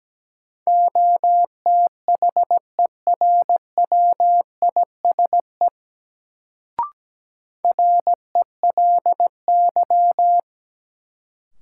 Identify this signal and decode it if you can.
Morse code, heard as a keyed tone, about 17 wpm, 700 Hz. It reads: OTHERWISE RELY
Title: Sentences from Top 1200 Words - New Word Drill - 17wpm